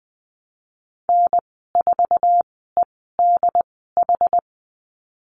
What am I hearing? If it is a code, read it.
Morse code, heard as a keyed tone, about 20 wpm, spaced as for 15 wpm, 700 Hz. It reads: N4EDH
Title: 088 Lesson 15g - Number 5 - Call Signs 20wpm